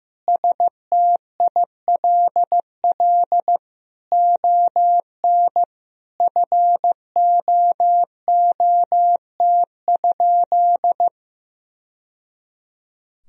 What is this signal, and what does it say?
Morse code, heard as a keyed tone, about 15 wpm, 700 Hz. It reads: STILL ON FOOT?
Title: Sentences from Top 1000 Words - Review Part 3 - 15wpm